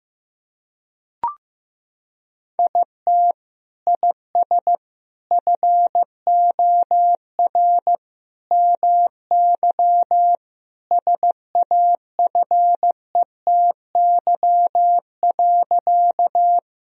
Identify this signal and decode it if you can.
Morse code, heard as a keyed tone, about 16 wpm, 700 Hz. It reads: IT IS FOR MY SAFETY.
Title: Sentences from Top 1000 Words - Review Part 4 - 15wpm